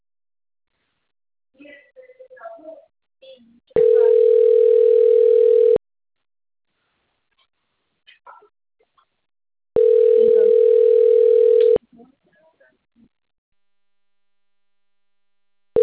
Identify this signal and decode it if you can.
Ringback tone (North America), 440 + 480 Hz.